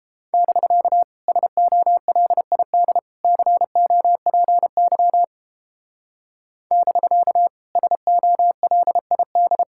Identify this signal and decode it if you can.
Morse code, heard as a keyed tone, about 31 wpm, 700 Hz. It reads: <BK> SOLID COPY <BK> SOLID